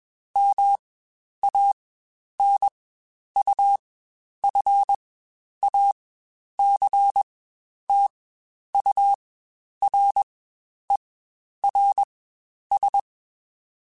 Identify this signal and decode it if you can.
Morse code, heard as a keyed tone, about 21 wpm, spaced as for 10 wpm, 800 Hz. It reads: MANUFACTURERS
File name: Quotes Spaced / Quotes Spaced-mp3s / Airtight sp @20